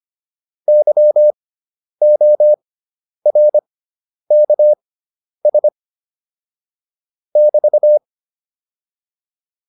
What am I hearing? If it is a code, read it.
Morse code, heard as a keyed tone, about 25 wpm, spaced as for 10 wpm, 600 Hz. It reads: YORKS <BT>